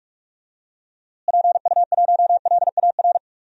Morse code, heard as a keyed tone, about 46 wpm, 700 Hz. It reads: PU1LAR
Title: DXCC-100 Call Signs - 6 Characters - Repeated 3x - 45wpm